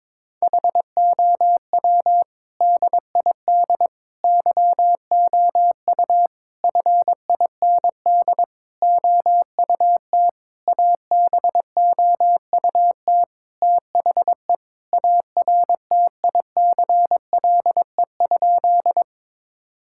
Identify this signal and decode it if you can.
Morse code, heard as a keyed tone, about 22 wpm, 700 Hz. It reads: HOW DID YOU FIND OUT ABOUT THE ARTICLE?
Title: Sentences from Top 1000 Words - Review Part 5 - 22wpm